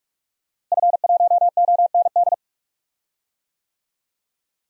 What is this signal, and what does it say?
Morse code, heard as a keyed tone, about 41 wpm, 700 Hz. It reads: F1OND
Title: DXCC-100 Call Signs - 5 Characters - Repeated 2x - 45wpm